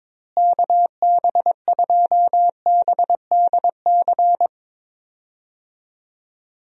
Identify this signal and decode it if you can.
Morse code, heard as a keyed tone, about 22 wpm, 700 Hz. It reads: KB2BDC
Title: USA Call Signs - 6 Characters - Repeated 2x - 22wpm